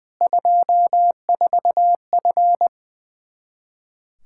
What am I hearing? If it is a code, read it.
Morse code, heard as a keyed tone, about 20 wpm, 700 Hz. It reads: 24F
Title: Practice QSO - Exchange 2 - Info - 20wpm